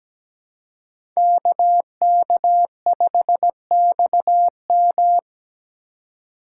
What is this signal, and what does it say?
Morse code, heard as a keyed tone, about 20 wpm, 700 Hz. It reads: KK5XM
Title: Speed Racing - USA 2x2 Call Signs 17wpm